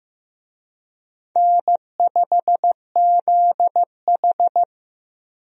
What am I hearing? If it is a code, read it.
Morse code, heard as a keyed tone, about 15 wpm, 700 Hz. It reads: N5ZH
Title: USA Call Signs - 4 Characters - Repeated 2x - 15wpm